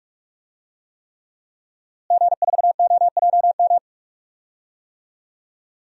Morse code, heard as a keyed tone, about 46 wpm, 700 Hz. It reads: G4OJM
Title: DXCC-100 Call Signs - 5 Characters - Repeated 3x - 45wpm